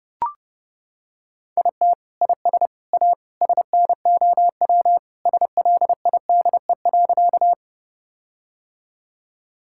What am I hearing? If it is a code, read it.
Morse code, heard as a keyed tone, about 30 wpm, 700 Hz. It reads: IT IS A SNOW SLIDE.
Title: Sentences from Top 1600 Words 30wpm